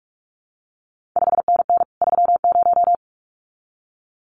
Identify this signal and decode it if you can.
Morse code, heard as a keyed tone, about 46 wpm, 700 Hz. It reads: 5NN 30
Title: CQ WW DX Contest Exchange Encore - 45wpm